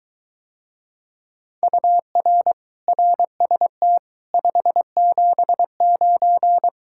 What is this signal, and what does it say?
Morse code, heard as a keyed tone, about 23 wpm, 700 Hz. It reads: UR RST 579